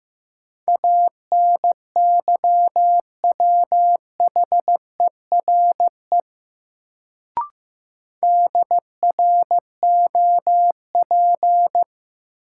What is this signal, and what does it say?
Morse code, heard as a keyed tone, about 15 wpm, 700 Hz. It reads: ANYWHERE DROP
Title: Sentences from Top 1400 Words - New Word Drill - 15wpm